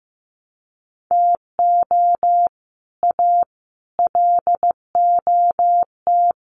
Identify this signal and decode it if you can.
Morse code, heard as a keyed tone, about 15 wpm, 700 Hz. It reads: TO A LOT
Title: Sets of 5 Words Encore 15wpm